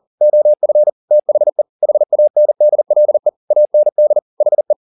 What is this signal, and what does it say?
Morse code, heard as a keyed tone, about 40 wpm, 600 Hz. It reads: OF THE HANDLE AND HE
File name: the old man and the sea 2 40wpm